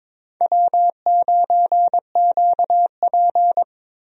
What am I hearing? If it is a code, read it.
Morse code, heard as a keyed tone, about 21 wpm, 700 Hz. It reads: W9QP
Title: USA Call Signs - 4 Characters - Repeated 3x - 22wpm